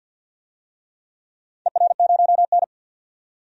Morse code, heard as a keyed tone, about 53 wpm, 700 Hz. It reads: ER0N
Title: DXCC-100 Call Signs - 4 Characters - 50wpm